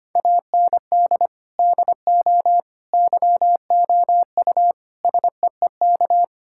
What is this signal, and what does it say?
Morse code, heard as a keyed tone, about 25 wpm, 700 Hz. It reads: AND DO YOU SEEK
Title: Sentences from Top 1600 Words 25wpm